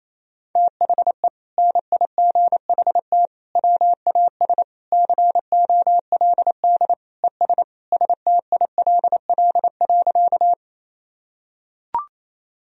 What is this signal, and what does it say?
Morse code, heard as a keyed tone, about 28 wpm, 700 Hz. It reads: THE NIGHT WAS COLD ES STILL.
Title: Sentences from Top 400 Words and Abbreviations 28wpm